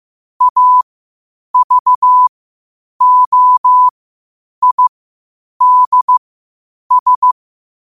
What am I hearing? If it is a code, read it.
Morse code, heard as a keyed tone, about 15 wpm, spaced as for 8 wpm, 1000 Hz. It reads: AVOIDS
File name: qotdfiles / 8wpm